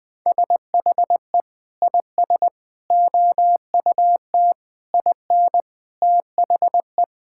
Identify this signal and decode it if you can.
Morse code, heard as a keyed tone, about 21 wpm, 700 Hz. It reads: SHE IS OUT IN THE
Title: Sentences with 3-Word Phrases - Spoken Beginning - 20wpm